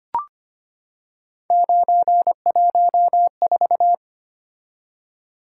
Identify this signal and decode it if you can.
Morse code, heard as a keyed tone, about 25 wpm, 700 Hz. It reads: 914
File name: Three Letter-Number Combinations - 25wpm